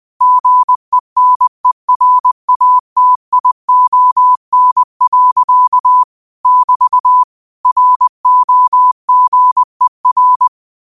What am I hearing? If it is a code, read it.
Morse code, heard as a keyed tone, about 20 wpm, 1000 Hz. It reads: GENERATION. <BT> ROGER
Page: Daily quotes in Morse Code at 20 Words per Minute. Use this to improve your code copying skill.